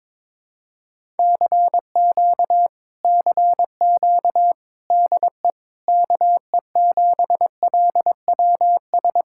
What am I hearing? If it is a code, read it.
Morse code, heard as a keyed tone, about 21 wpm, 700 Hz. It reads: CQ CQ DE KE7LWS
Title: Speed-Racing Common QSO Phrases - 22wpm